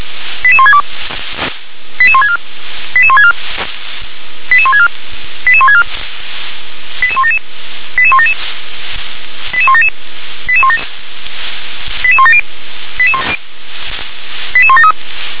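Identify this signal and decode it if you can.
Five-tone selective call (ZVEI), heard as quick runs of five tones, tones of about 70 ms. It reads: 88162 88164 88164 88164 88164 88177 88177 88177 88170 88173
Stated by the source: ZVEI five-tone selective call received by SDR.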